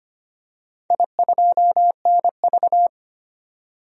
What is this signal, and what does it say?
Morse code, heard as a keyed tone, about 25 wpm, 700 Hz. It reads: I2NV
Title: DXCC-100 Call Signs - 4 Characters - 25wpm